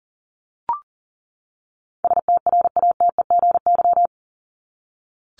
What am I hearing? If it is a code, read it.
Morse code, heard as a keyed tone, about 40 wpm, 700 Hz. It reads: STRATEGY
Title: Top 501-750 Words 40wpm